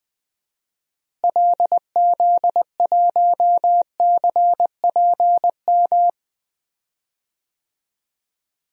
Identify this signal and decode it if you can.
Morse code, heard as a keyed tone, about 21 wpm, 700 Hz. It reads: LZ1CPM
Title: DXCC-100 Call Signs - 6 Characters - Repeated 3x - 20wpm